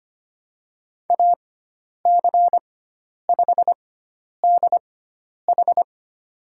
Morse code, heard as a keyed tone, about 25 wpm, spaced as for 10 wpm, 700 Hz. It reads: AC5DH
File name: Lesson 16g - Letter c - Call Signs 25wpm